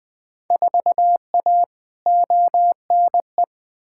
Morse code, heard as a keyed tone, about 20 wpm, 700 Hz. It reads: 4A ONE
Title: Field Day Exchanges Encore - 20wpm